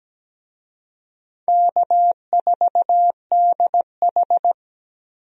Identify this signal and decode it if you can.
Morse code, heard as a keyed tone, about 17 wpm, 700 Hz. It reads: K4DH
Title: USA Call Signs - 4 Characters - 17wpm